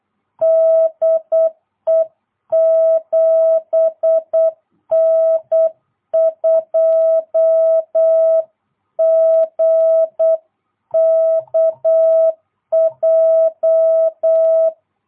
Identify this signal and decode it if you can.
Morse code, heard as a keyed tone, about 8 wpm, 660 Hz. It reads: DE 7N2GKJ